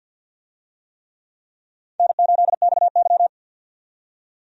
Morse code, heard as a keyed tone, about 53 wpm, 700 Hz. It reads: N8XY
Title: USA Call Signs - 4 Characters - 50wpm